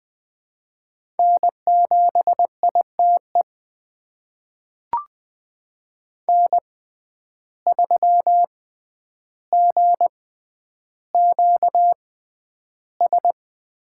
Morse code, heard as a keyed tone, about 20 wpm, spaced as for 10 wpm, 700 Hz. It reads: N7ITE N3GQS
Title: USA Call Signs - 5 Characters - 6x Character Spacing - 20wpm